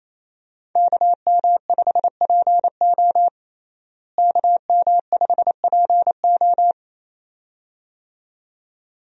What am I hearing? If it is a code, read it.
Morse code, heard as a keyed tone, about 28 wpm, 700 Hz. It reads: KM5PO KM5PO
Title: USA Call Signs - 5 Characters - Repeated 2x - 28wpm